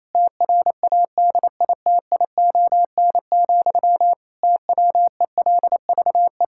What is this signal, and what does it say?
Morse code, heard as a keyed tone, about 28 wpm, 700 Hz. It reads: TRADITION, TWELVE
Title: Pangram Challenge v2 - 28wpm